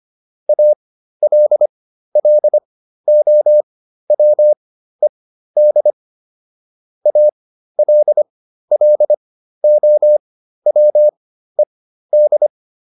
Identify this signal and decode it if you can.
Morse code, heard as a keyed tone, about 25 wpm, spaced as for 13 wpm, 600 Hz. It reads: ALLOWED ALLOWED